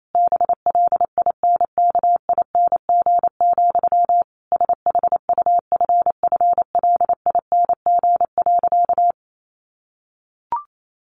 Morse code, heard as a keyed tone, about 28 wpm, 700 Hz. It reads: BLINKING, SHUFFLING.
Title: A Sound Of Thunder 28wpm